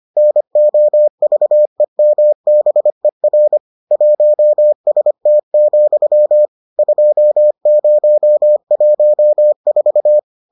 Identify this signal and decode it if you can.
Morse code, heard as a keyed tone, about 25 wpm, 600 Hz. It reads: NOVEMBER 1ST, 2014